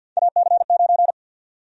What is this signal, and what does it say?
Morse code, heard as a keyed tone, about 52 wpm, 700 Hz. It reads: AC9
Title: USA Call Sign Prefixes - 50wpm